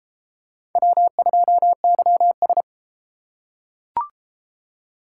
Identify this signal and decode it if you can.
Morse code, heard as a keyed tone, about 28 wpm, 700 Hz. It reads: W2YS